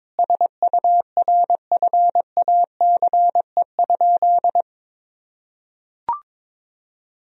Morse code, heard as a keyed tone, about 22 wpm, 700 Hz. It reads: SURFACE?
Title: Sentences from Top 1000 Words - Review Part 3 - 22wpm